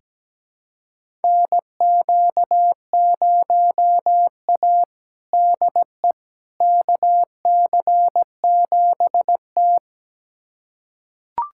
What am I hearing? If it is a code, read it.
Morse code, heard as a keyed tone, about 17 wpm, 700 Hz. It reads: NQ0A DE KC7T
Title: USA Callsign DE Callsign - 4 Characters - 17wpm